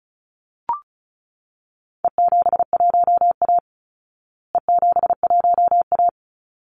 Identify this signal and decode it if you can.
Morse code, heard as a keyed tone, about 34 wpm, 700 Hz. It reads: E71A E71A
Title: Call Signs Repeated 3x - 35wpm